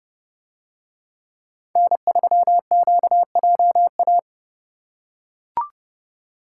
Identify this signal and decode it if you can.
Morse code, heard as a keyed tone, about 29 wpm, 700 Hz. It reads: N3QJA